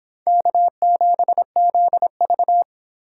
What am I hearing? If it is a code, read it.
Morse code, heard as a keyed tone, about 26 wpm, 700 Hz. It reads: K7ZV